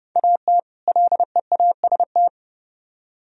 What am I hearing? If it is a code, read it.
Morse code, heard as a keyed tone, about 35 wpm, 700 Hz. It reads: AT LEAST
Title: Speed-Racing - Sets of 2 Words 30wpm